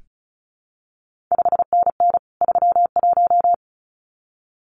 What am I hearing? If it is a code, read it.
Morse code, heard as a keyed tone, about 36 wpm, 700 Hz. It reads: 5NN 31
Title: CQ WW DX Contest Exchange Encore - 35wpm